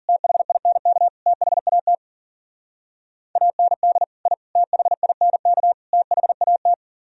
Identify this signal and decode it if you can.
Morse code, heard as a keyed tone, about 54 wpm, 700 Hz. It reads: THINK THAT AND I THINK THAT